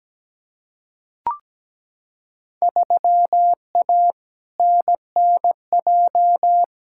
Morse code, heard as a keyed tone, about 17 wpm, 700 Hz. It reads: 3A NNJ
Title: Field Day Exchanges Encore - 17wpm